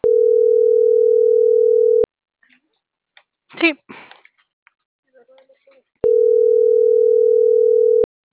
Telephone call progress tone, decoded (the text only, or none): ringback tone